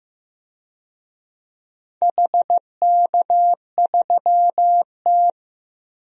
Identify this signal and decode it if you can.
Morse code, heard as a keyed tone, about 15 wpm, 700 Hz. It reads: HK3T
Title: DXCC-100 Call Signs - 4 Characters - 15wpm